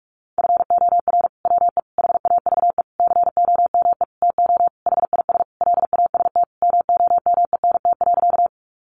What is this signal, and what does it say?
Morse code, heard as a keyed tone, about 44 wpm, 700 Hz. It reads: FOR WE HAVE COME TO HIS LAST MOMENT.
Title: Sentences from Top 1000 Words 45wpm